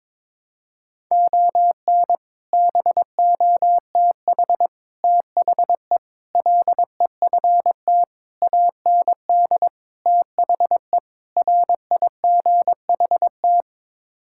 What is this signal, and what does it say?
Morse code, heard as a keyed tone, about 22 wpm, 700 Hz. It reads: ON BOTH THE LEFT AND THE RIGHT
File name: practice / 22wpm / Sets of 7 Words - 22wpm